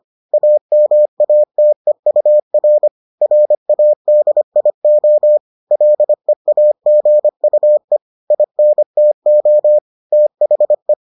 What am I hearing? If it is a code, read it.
Morse code, heard as a keyed tone, about 25 wpm, 600 Hz. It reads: AMATEUR RADIO LEAGUE INTO THE